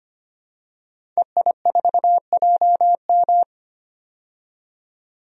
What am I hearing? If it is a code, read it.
Morse code, heard as a keyed tone, about 25 wpm, 700 Hz. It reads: EI4JM
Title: DXCC-100 Call Signs - 5 Characters - 25wpm